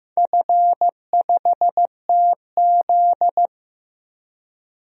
Morse code, heard as a keyed tone, about 15 wpm, 700 Hz. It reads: F5TZ
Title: DXCC-100 Call Signs - 4 Characters - 15wpm